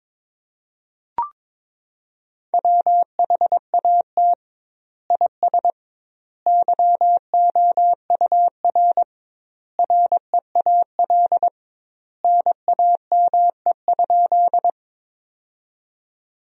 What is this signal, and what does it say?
Morse code, heard as a keyed tone, about 22 wpm, 700 Hz. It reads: WHAT IS YOUR REAL NAME?
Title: 2x Word-Spacing - Sentences from Top 300 Words - 22wpm